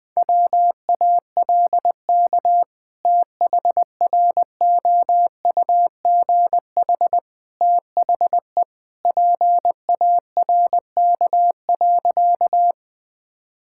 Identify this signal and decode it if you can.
Morse code, heard as a keyed tone, about 20 wpm, 700 Hz. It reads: WALK THROUGH THE PARK.
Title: Sentences with 2-Word Phrases - Spoken Beginning - 20wpm